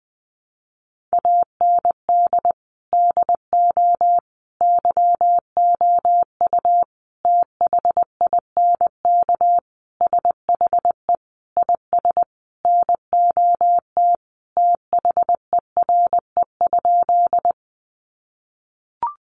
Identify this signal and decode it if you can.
Morse code, heard as a keyed tone, about 20 wpm, 700 Hz. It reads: AND DO YOU THINK SHE IS NOT THERE?
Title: Sentences from Top 600 Words 20wpm